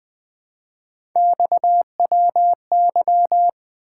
Morse code, heard as a keyed tone, about 20 wpm, 700 Hz. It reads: XWY
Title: Three Letter-Number Combinations - 20wpm